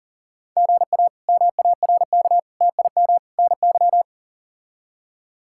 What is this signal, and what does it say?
Morse code, heard as a keyed tone, about 38 wpm, 700 Hz. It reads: GA MARK TIM NY
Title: K1USN SST QSO Practice - 40wpm